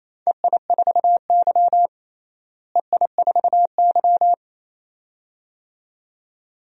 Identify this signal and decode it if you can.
Morse code, heard as a keyed tone, about 28 wpm, 700 Hz. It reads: EI4Y EI4Y
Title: DXCC-100 Call Signs - 4 Characters - Repeated 2x - 28wpm